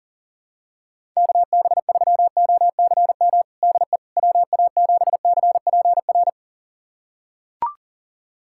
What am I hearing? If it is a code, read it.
Morse code, heard as a keyed tone, about 40 wpm, 700 Hz. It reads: KB3OCM DE WA7CPR
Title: USA Callsign DE Callsign 40wpm